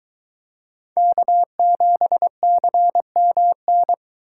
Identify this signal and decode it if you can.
Morse code, heard as a keyed tone, about 23 wpm, 700 Hz. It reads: K7CMN